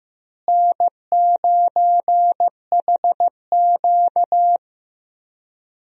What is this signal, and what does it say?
Morse code, heard as a keyed tone, about 15 wpm, 700 Hz. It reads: N9HQ
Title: USA Call Signs - 4 Characters - Repeated 2x - 15wpm